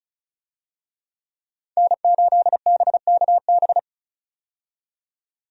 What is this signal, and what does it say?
Morse code, heard as a keyed tone, about 34 wpm, 700 Hz. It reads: N8BKB
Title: USA Call Signs - 5 Characters - 35wpm